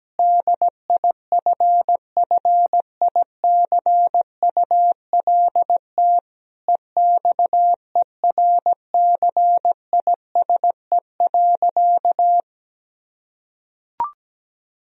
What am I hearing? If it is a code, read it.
Morse code, heard as a keyed tone, about 17 wpm, 700 Hz. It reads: DIFFICULT EXERCISE.
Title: Sentences from Top 1000 Words - Review Part 4 - 17wpm